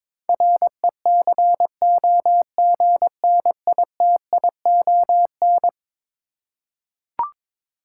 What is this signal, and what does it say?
Morse code, heard as a keyed tone, about 22 wpm, 700 Hz. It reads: RECOGNITION
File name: practice / 22wpm / Eleven Letter Words - 22wpm